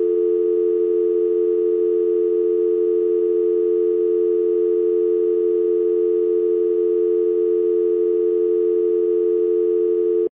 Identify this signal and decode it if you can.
Dial tone (North America), 350 + 440 Hz.
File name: Scratches (Windows) (gamerip) (2006)